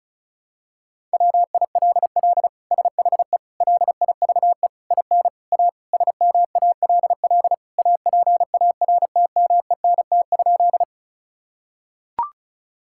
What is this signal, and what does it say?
Morse code, heard as a keyed tone, about 36 wpm, 700 Hz. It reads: WILL SHE LIVE IN A SMALL APARTMENT?
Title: Sentences from Top 1000 Words - Review Part 6 - 35wpm